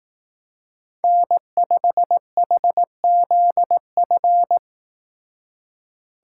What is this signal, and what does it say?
Morse code, heard as a keyed tone, about 18 wpm, 700 Hz. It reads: N5HZF
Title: Speed Racing - USA 1x3 Call Signs 15wpm